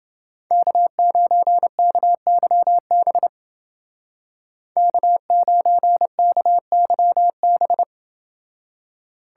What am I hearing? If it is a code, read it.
Morse code, heard as a keyed tone, about 28 wpm, 700 Hz. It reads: K9KYB K9KYB